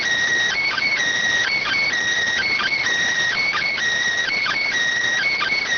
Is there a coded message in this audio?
Yes: Emergency Alert System (SAME) tones from 0.5 to 4.7 s, heard as harsh buzzing bursts.